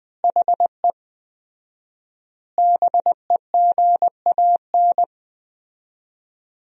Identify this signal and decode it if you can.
Morse code, heard as a keyed tone, about 20 wpm, 700 Hz. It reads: HE BEGAN
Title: 4x Word-Spacing - Sentences from Top 300 Words - 20wpm